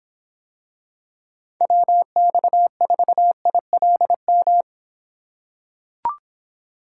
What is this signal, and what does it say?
Morse code, heard as a keyed tone, about 25 wpm, 700 Hz. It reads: WX4ILM